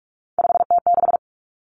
Morse code, heard as a keyed tone, about 46 wpm, 700 Hz. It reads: 5T6
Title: Three Challenging Letter-Number Combinations 45wpm